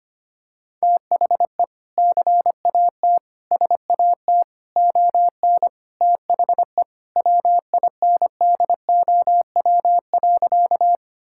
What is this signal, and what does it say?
Morse code, heard as a keyed tone, about 25 wpm, 700 Hz. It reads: THE CAT SAT ON THE WINDOW.
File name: practice / 25wpm / Sentences with 2-Word Phrases - 25wpm